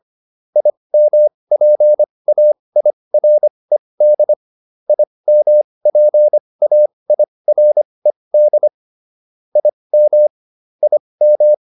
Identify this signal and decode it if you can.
Morse code, heard as a keyed tone, about 25 wpm, spaced as for 20 wpm, 600 Hz. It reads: IMPAIRED IMPAIRED IM IM